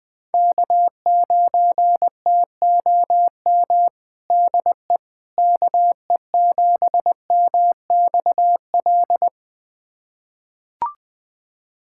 Morse code, heard as a keyed tone, about 21 wpm, 700 Hz. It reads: K9TOM DE KE7MXL
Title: USA Callsign DE Callsign 20wpm